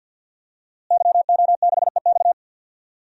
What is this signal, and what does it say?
Morse code, heard as a keyed tone, about 46 wpm, 700 Hz. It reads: YO6EX
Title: DXCC-100 Call Signs - 5 Characters - Repeated 3x - 50wpm